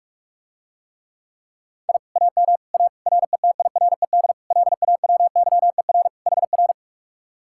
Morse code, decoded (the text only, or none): I AM A RETIRED LAWYER HR